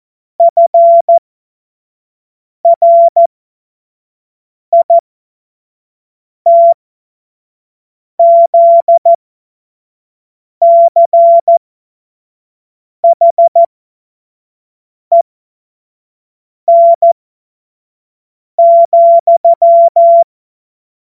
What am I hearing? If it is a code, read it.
Morse code, heard as a keyed tone, about 14 wpm, spaced as for 5 wpm, 680 Hz. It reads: FRITZCHEN,
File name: Morsekurs / Witze / Witz 14. September 2021, 25 F70